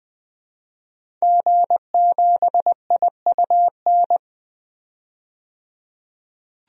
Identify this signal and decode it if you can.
Morse code, heard as a keyed tone, about 20 wpm, 700 Hz. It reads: G7IUN
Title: DXCC-100 Call Signs - 5 Characters - 20wpm